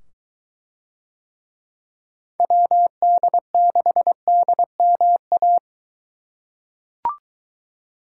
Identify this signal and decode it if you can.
Morse code, heard as a keyed tone, about 23 wpm, 700 Hz. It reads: WD6DMA